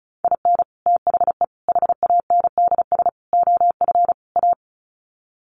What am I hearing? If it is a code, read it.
Morse code, heard as a keyed tone, about 34 wpm, 700 Hz. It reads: IN THE HANDS OF A
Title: Sets of 5 Words Encore 35wpm